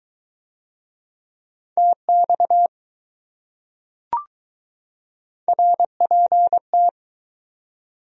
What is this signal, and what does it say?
Morse code, heard as a keyed tone, about 20 wpm, 700 Hz. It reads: TX RPT